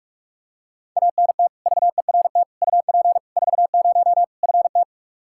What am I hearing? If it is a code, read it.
Morse code, heard as a keyed tone, about 46 wpm, 700 Hz. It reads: ANT VERT UP 40 FT
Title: Practice QSO - Exchange 2 - Info - 45wpm